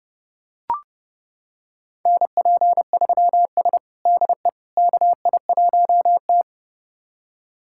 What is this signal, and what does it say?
Morse code, heard as a keyed tone, about 30 wpm, 700 Hz. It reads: NP3S DE KI1T
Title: USA Callsign DE Callsign - 4 Characters - 30wpm